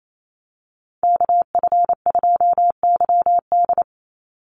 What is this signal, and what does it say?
Morse code, heard as a keyed tone, about 28 wpm, 700 Hz. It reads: KF2YD
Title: USA Call Signs 28wpm